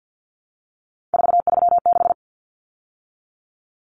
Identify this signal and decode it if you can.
Morse code, heard as a keyed tone, about 50 wpm, 700 Hz. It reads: 436
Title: Numbers - 3-Digit 50wpm